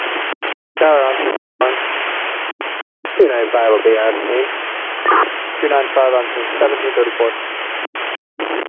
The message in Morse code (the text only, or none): none